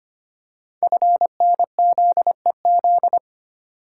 Morse code, heard as a keyed tone, about 25 wpm, 700 Hz. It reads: FNZEZ
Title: Code Groups - 5 Letters - 25wpm